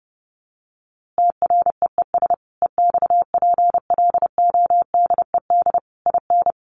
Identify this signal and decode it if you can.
Morse code, heard as a keyed tone, about 30 wpm, 700 Hz. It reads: TREES EXPLODED IN
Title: A Sound Of Thunder 30wpm